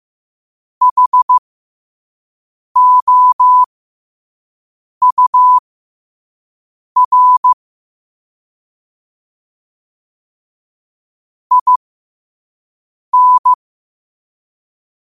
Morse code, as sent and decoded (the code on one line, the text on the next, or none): .... --- ..- .-. / .. -.
HOUR IN